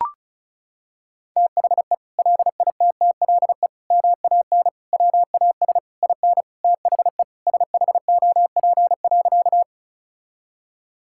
Morse code, heard as a keyed tone, about 34 wpm, 700 Hz. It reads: THE LITTLE MAN WAS IN THE SHOP.
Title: Sentences from Top 1600 Words 35wpm